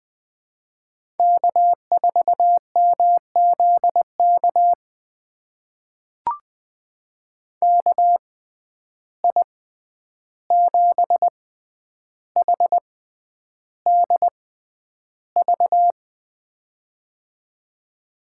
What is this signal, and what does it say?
Morse code, heard as a keyed tone, about 21 wpm, spaced as for 10 wpm, 700 Hz. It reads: K4MZK KI7HDV
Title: USA Call Signs - 6x Character Spacing - 20wpm